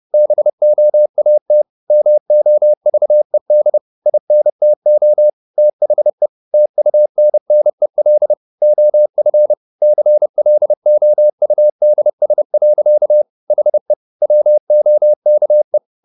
Morse code, heard as a keyed tone, about 30 wpm, 600 Hz. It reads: BOAT MOVED INTO THE TUNNEL OF CLOUDS. HE WOKE